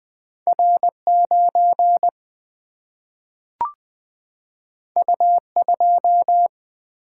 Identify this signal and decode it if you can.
Morse code, heard as a keyed tone, about 20 wpm, 700 Hz. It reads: R 9 U 2
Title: Single-Letter Single-Number 20wpm